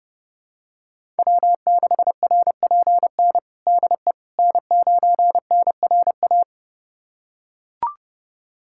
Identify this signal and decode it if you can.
Morse code, heard as a keyed tone, about 32 wpm, 700 Hz. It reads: W6RPN DE N9NRA
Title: USA Callsign DE Callsign - 5 Characters - 30wpm